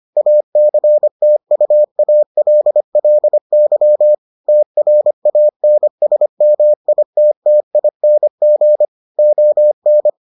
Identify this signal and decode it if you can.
Morse code, heard as a keyed tone, about 25 wpm, 600 Hz. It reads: ACTUALLY TRANSMITTING ON